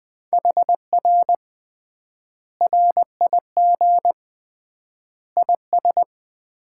Morse code, heard as a keyed tone, about 20 wpm, 700 Hz. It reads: HR RIG IS
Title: Common QSO Phrases - 3x Word-Spacing - 20wpm